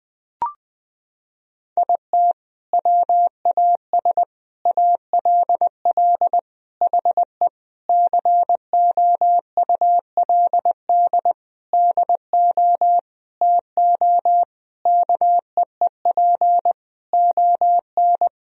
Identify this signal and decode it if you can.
Morse code, heard as a keyed tone, about 21 wpm, 700 Hz. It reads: IT WAS ALL HE COULD DO TO KEEP ON
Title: Sentences from Top 1000 Words - Review Part 1 - 20wpm